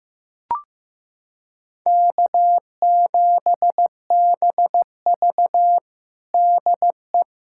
Morse code, heard as a keyed tone, about 15 wpm, 700 Hz. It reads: K7BV DE
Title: K1USN SST QSO Practice 15wpm